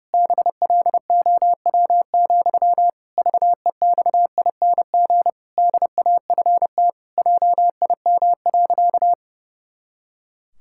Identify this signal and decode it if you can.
Morse code, heard as a keyed tone, about 28 wpm, 700 Hz. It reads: BLOW, VEXING DAFT JIM.
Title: Pangram Challenge v2 - 30wpm